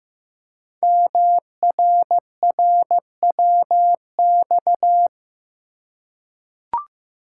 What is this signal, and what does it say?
Morse code, heard as a keyed tone, about 15 wpm, 700 Hz. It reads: MRRWX
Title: Code Groups - 5 Letters - 15wpm